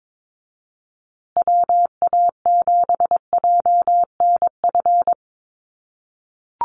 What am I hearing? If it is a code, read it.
Morse code, heard as a keyed tone, about 22 wpm, 700 Hz. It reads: WA7JNF
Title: USA Call Signs Repeated 3x 22wpm